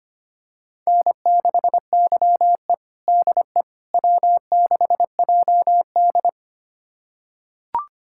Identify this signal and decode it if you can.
Morse code, heard as a keyed tone, about 25 wpm, 700 Hz. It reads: N6YE DE W6JD
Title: USA Callsign DE Callsign - 4 Characters - 25wpm